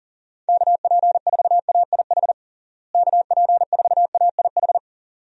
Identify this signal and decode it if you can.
Morse code, heard as a keyed tone, about 40 wpm, 700 Hz. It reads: KP4AIH KP4AIH
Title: DXCC-100 Call Signs - 6 Characters - Repeated 2x - 40wpm